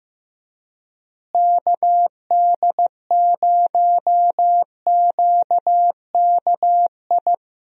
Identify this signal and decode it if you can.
Morse code, heard as a keyed tone, about 15 wpm, 700 Hz. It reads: KD0QKI
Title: USA Call Signs - 6 Characters - Repeated 2x - 15wpm